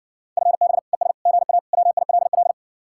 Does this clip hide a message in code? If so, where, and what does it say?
Yes: Morse code from 0.4 to 2.5 s, heard as a keyed tone, about 60 wpm, 700 Hz. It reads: FB ES GA WILL